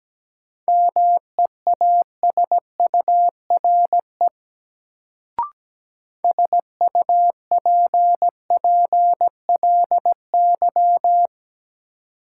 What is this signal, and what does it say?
Morse code, heard as a keyed tone, about 17 wpm, 700 Hz. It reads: MEASURE SUPPLY
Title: Sentences from Top 1200 Words - New Word Drill Alt - 17wpm